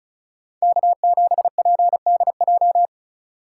Morse code, heard as a keyed tone, about 36 wpm, 700 Hz. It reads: K7PDJ
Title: USA Call Signs - 5 Characters - 2x Character Spacing - 35wpm